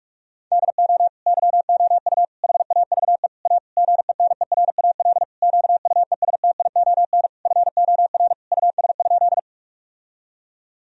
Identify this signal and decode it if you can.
Morse code, heard as a keyed tone, about 46 wpm, 700 Hz. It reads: DO YOU HAVE A GENERAL QUESTION FOR US?
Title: Sentences from Top 1000 Words - Review Part 3 - 45wpm